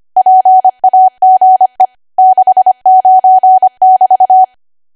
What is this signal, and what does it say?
Morse code, heard as a keyed tone, about 25 wpm, 750 Hz. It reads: PAGE 69<BT>
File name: cwa / W1AW-Level3 / 25WPM